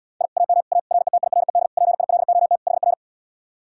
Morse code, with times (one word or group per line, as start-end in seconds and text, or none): 0.2-0.3 I
0.4-0.6 AM
0.7-0.8 A
0.9-1.7 RETIRED
1.8-2.6 PILOT
2.7-2.9 HR